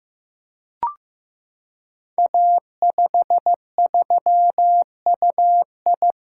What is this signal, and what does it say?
Morse code, heard as a keyed tone, about 15 wpm, 700 Hz. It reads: A53UI
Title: Code Groups - 5 Letter:Numbers - 15wpm